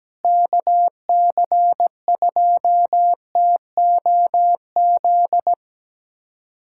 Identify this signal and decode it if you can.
Morse code, heard as a keyed tone, about 17 wpm, 700 Hz. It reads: KC2TOZ
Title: USA Callsign DE Callsign 17wpm